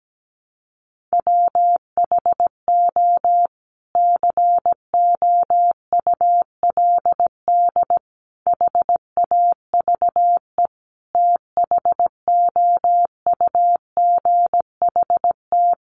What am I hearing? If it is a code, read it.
Morse code, heard as a keyed tone, about 17 wpm, 700 Hz. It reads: WHO COULD HAVE THOUGHT
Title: Sentences from Top 300 Words 17wpm